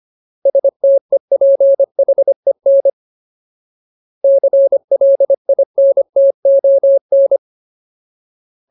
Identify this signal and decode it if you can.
Morse code, heard as a keyed tone, about 25 wpm, 550 Hz. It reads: STEPHEN CLINTON
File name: Ultimate Word List / 25WPM Ultimate